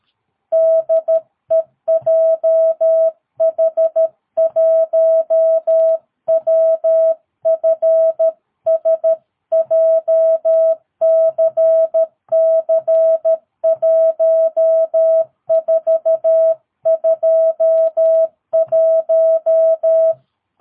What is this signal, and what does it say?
Morse code, heard as a keyed tone, about 13 wpm, 650 Hz. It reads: DE JH1WFS JCC1421
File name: jh8cbh / 2022cw